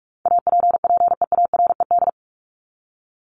Morse code, heard as a keyed tone, about 44 wpm, 700 Hz. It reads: APPEARED
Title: ED Suffix - Top 100 Words with Base Words 45wpm